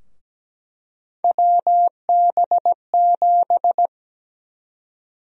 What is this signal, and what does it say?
Morse code, heard as a keyed tone, about 17 wpm, 700 Hz. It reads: WB7
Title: USA Call Sign Prefixes - Repeated 2x - 17wpm